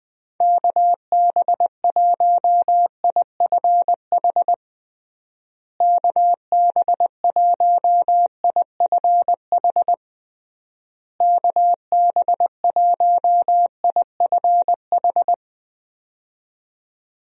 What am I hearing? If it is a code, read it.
Morse code, heard as a keyed tone, about 20 wpm, 700 Hz. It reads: KB1IFH KB1IFH KB1IFH
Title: USA Call Signs - 6 Characters - Repeated 3x - 20wpm